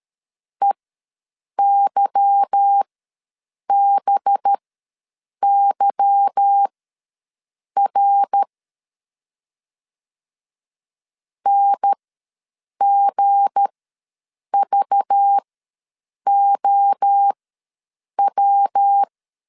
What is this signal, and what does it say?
Morse code, heard as a keyed tone, about 13 wpm, spaced as for 7 wpm, 800 Hz. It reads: EYBYR NGVOW
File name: LEZIONI / 8 YJ45 / 8 Y ws 12wpm 3409